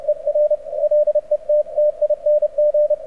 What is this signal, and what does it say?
Morse code, heard as a keyed tone, about 31 wpm, 590 Hz. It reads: ER LETTING
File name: archived / CRSnet